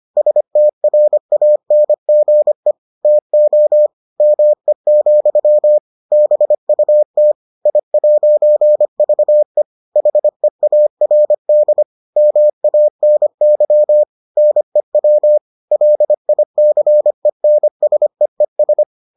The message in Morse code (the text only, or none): STRANGE TO ME, BUT I'VE HEARD MANY NEW LICENSEES